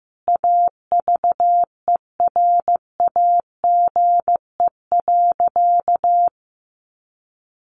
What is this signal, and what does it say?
Morse code, heard as a keyed tone, about 15 wpm, 700 Hz. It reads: AVERAGE.
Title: Sentences from Top 1400 Words 15wpm